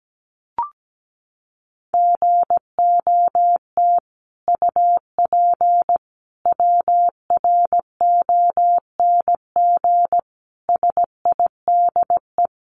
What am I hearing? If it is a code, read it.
Morse code, heard as a keyed tone, about 17 wpm, 700 Hz. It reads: GOT UP WRONG SIDE
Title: Sentences from Top 900 Words 17wpm